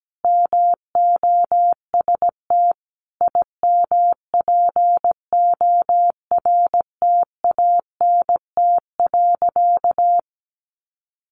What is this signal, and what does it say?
Morse code, heard as a keyed tone, about 17 wpm, 700 Hz. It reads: MOST IMPORTANT.
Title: Sentences from Top 1200 Words 17wpm